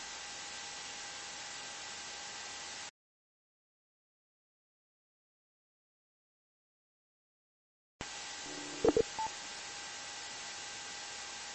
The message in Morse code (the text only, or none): none